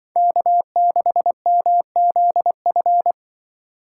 Morse code, heard as a keyed tone, about 24 wpm, 700 Hz. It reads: K6MZF
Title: Speed Racing - USA 1x3 Call Signs 20wpm